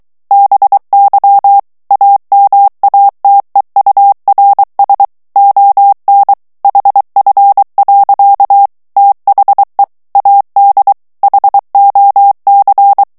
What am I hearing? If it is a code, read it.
Morse code, heard as a keyed tone, about 23 wpm, 800 Hz. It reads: BY AMATEURS ON HF. THE AD HOC
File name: ANGLAIS texte clair 3 vitesse 22 mots minute